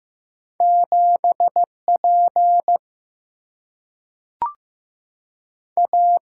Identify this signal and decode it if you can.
Morse code, heard as a keyed tone, about 15 wpm, 700 Hz. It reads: 7 P A
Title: Single-Letter Single-Number 15wpm